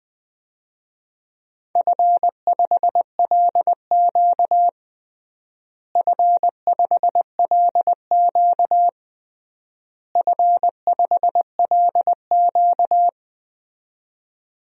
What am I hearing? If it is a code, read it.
Morse code, heard as a keyed tone, about 20 wpm, 700 Hz. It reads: F5LQ F5LQ F5LQ
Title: DXCC-100 Call Signs - 4 Characters - Repeated 3x - 20wpm